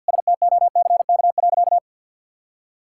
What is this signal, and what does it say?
Morse code, heard as a keyed tone, about 50 wpm, 700 Hz. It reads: STOCK.
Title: Sentences from Top 1000 Words - Review Part 5 - 50wpm